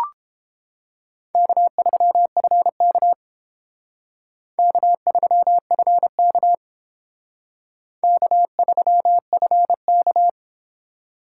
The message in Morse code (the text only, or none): K3FK K3FK K3FK